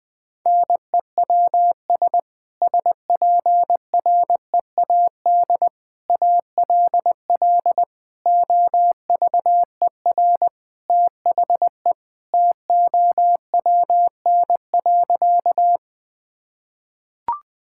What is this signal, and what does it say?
Morse code, heard as a keyed tone, about 20 wpm, 700 Hz. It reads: NEWS SPREAD ALL OVER THE TOWN.
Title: Sentences with 3-Word Phrases - 20wpm